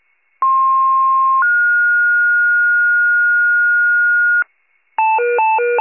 Two-tone page, heard as a one-second tone, then a three-second tone, about 1060 then 1430 Hz, with noise. It is heard from 0.4 to 4.4 s.
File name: wcalert